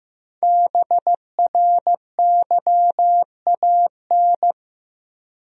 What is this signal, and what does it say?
Morse code, heard as a keyed tone, about 15 wpm, 700 Hz. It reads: BRYAN
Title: USA Common Names - 15wpm